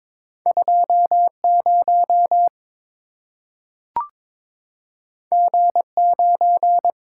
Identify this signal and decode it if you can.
Morse code, heard as a keyed tone, about 22 wpm, 700 Hz. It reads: 20 G9
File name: practice / 22wpm / Two Letter-Number Combinations - 22wpm